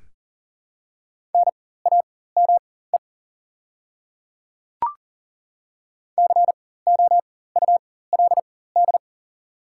Morse code, heard as a keyed tone, about 40 wpm, spaced as for 19 wpm, 690 Hz. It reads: NAME COULD
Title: 4x Character Spacing - Top 200 Words - 40wpm